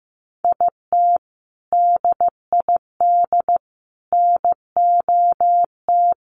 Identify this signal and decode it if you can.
Morse code, heard as a keyed tone, about 15 wpm, 700 Hz. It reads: IT DID NOT
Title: Sentences from Top 1500 Words 15wpm